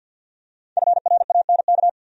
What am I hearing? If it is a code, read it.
Morse code, heard as a keyed tone, about 48 wpm, 700 Hz. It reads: FRANK
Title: USA Common Names - 50wpm